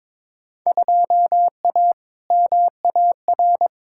Morse code, heard as a keyed tone, about 21 wpm, 700 Hz. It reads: 2A MAR
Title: Field Day Exchanges - 22wpm